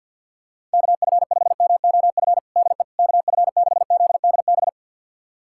Morse code, heard as a keyed tone, about 50 wpm, 700 Hz. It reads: KF5MOF DE KF6ZDB
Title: USA Callsign DE Callsign - 6 Characters - 50wpm